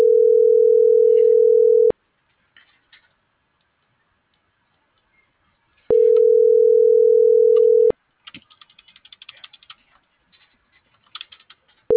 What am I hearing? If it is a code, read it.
Ringback tone (North America), 440 + 480 Hz.